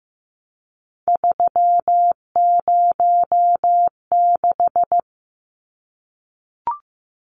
Morse code, heard as a keyed tone, about 15 wpm, 700 Hz. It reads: 306
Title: Numbers - 3-Digit 15wpm